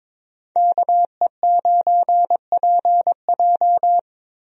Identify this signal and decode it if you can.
Morse code, heard as a keyed tone, about 21 wpm, 700 Hz. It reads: KE9PJ
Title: USA Call Signs - 5 Characters - Repeated 2x - 22wpm